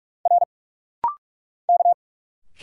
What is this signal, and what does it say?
Morse code, heard as a keyed tone, about 46 wpm, 700 Hz. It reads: R K
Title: Single Letters - Rapid-Fire 45wpm